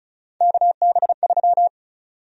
Morse code, heard as a keyed tone, about 36 wpm, 700 Hz. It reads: KB3
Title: USA Call Sign Prefixes - Repeated 2x - 35wpm